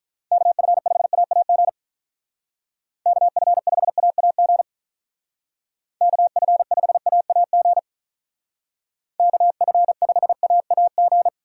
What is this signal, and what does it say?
Morse code, heard as a keyed tone, about 44 wpm, 700 Hz. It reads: KF5AAG KF5AAG KF5AAG KF5AAG